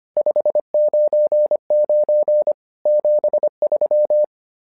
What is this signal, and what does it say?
Morse code, heard as a keyed tone, about 25 wpm, 600 Hz. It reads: 599 73
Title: SOTA QSO Practice 25wpm